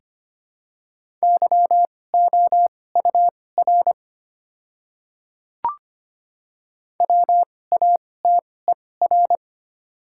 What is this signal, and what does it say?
Morse code, heard as a keyed tone, about 25 wpm, spaced as for 18 wpm, 690 Hz. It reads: YOUR WATER
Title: 2x Character Spacing - Top 100 Words - 25wpm